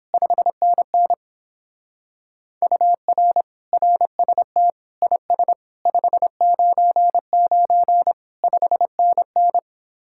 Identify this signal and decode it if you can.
Morse code, heard as a keyed tone, about 31 wpm, 700 Hz. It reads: 5NN UR RST IS 599 5NN